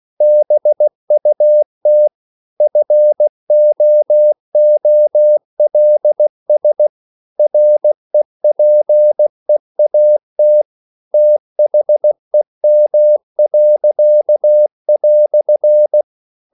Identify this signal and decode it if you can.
Morse code, heard as a keyed tone, about 16 wpm, 600 Hz. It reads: BUT FOOLS REPEAT THEM."
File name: CW / Practice / EnglishQuotations0216